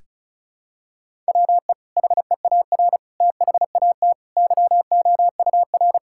Morse code, heard as a keyed tone, about 34 wpm, 700 Hz. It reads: WE HEAR THAT YOUR
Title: Sentences from Top 1000 Words - Review Part 5 - 35wpm